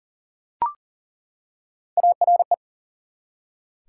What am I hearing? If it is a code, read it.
Morse code, heard as a keyed tone, about 38 wpm, 700 Hz. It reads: ARE
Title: Three Letter Words - 40wpm